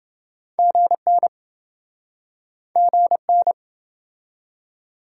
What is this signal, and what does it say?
Morse code, heard as a keyed tone, about 28 wpm, 700 Hz. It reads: GN GN